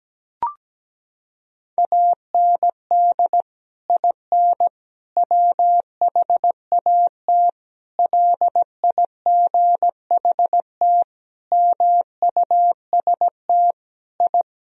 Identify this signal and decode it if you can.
Morse code, heard as a keyed tone, about 17 wpm, 700 Hz. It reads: AND IN WHAT LIGHT MUST I
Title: Sentences from Top 1000 Words - Review Part 2 - 17wpm